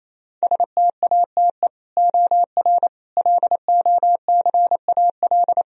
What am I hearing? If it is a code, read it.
Morse code, heard as a keyed tone, about 28 wpm, 700 Hz. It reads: STATE OR LOCAL
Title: Sentences from Top 1000 Words - Review Part 3 - 28wpm